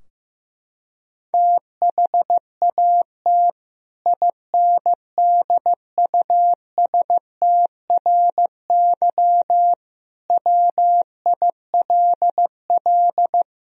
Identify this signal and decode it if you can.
Morse code, heard as a keyed tone, about 15 wpm, 700 Hz. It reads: THAT INDUSTRY WILL
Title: Sentences from Top 1000 Words - Review Part 4 - 15wpm